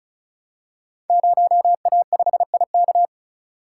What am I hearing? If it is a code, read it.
Morse code, heard as a keyed tone, about 36 wpm, 690 Hz. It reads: 0A5IK
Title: Code Groups - 5 Letter:Numbers - 35wpm